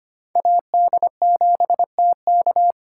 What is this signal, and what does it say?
Morse code, heard as a keyed tone, about 25 wpm, 700 Hz. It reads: AD7TK
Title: USA Call Signs - 4x Character Spacing - 25wpm